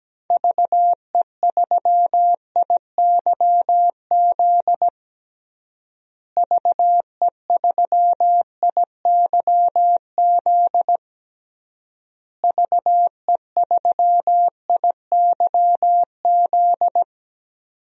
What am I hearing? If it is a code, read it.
Morse code, heard as a keyed tone, about 17 wpm, 700 Hz. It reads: VE3IYZ VE3IYZ VE3IYZ
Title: DXCC-100 Call Signs - 6 Characters - Repeated 3x - 17wpm